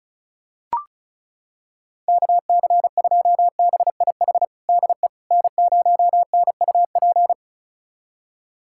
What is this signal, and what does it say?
Morse code, heard as a keyed tone, about 36 wpm, 700 Hz. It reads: KC2BIH DE N0NUP
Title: USA Callsign DE Callsign 35wpm